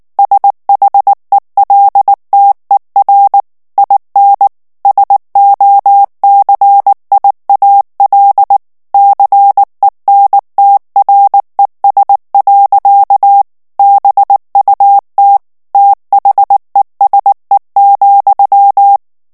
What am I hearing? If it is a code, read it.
Morse code, heard as a keyed tone, about 19 wpm, 800 Hz. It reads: SHELTER IN SOCIAL CENTRES. BUT THESE,
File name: morse mp3 / ANGLAIS texte clair 2 vitesse 18 mots minute